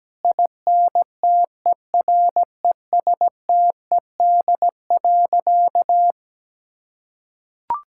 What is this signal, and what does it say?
Morse code, heard as a keyed tone, about 17 wpm, 700 Hz. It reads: INTERESTED.
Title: Sentences from Top 1000 Words - Review Part 6 - 17wpm